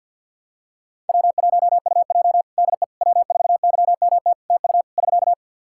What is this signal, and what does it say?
Morse code, heard as a keyed tone, about 51 wpm, 700 Hz. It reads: W1UJ DE W4YMT TU <SK>
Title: Practice QSO - Exchange 4 - Ending - 50wpm